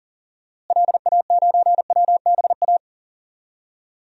Morse code, heard as a keyed tone, about 40 wpm, 700 Hz. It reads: LA9WBA
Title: DXCC-100 Call Signs - 6 Characters - Repeated 3x - 40wpm